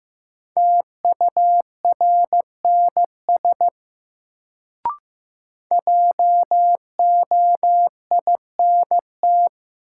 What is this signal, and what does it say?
Morse code, heard as a keyed tone, about 15 wpm, 700 Hz. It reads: TURNS JOINT
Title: Sets of 5 Words Encore - New Word Drill - 15wpm